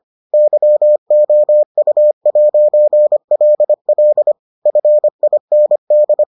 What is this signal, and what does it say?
Morse code, heard as a keyed tone, about 25 wpm, 600 Hz. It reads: YOU'LL FIND